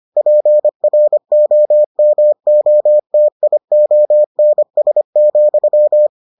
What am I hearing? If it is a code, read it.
Morse code, heard as a keyed tone, about 25 wpm, 600 Hz. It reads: PROMOTIONS,